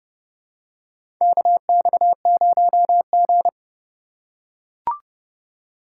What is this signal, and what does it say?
Morse code, heard as a keyed tone, about 30 wpm, 700 Hz. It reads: KX0G
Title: USA Call Signs - 4 Characters - 3x Character Spacing - 30wpm